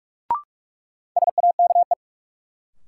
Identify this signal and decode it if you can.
Morse code, heard as a keyed tone, about 41 wpm, 700 Hz. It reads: SAKE
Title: Sets of 4 Words - New Word Drill - 45wpm